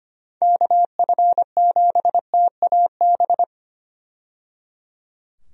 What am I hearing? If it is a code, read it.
Morse code, heard as a keyed tone, about 25 wpm, 700 Hz. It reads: KF7TAB
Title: USA Call Signs - 6 Characters - 25wpm